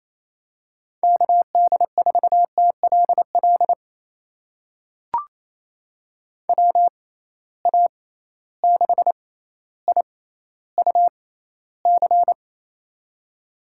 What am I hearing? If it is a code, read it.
Morse code, heard as a keyed tone, about 28 wpm, spaced as for 14 wpm, 700 Hz. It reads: KD4TLL WA6IUC
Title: USA Call Signs - 6 Characters - 6x Character Spacing - 28wpm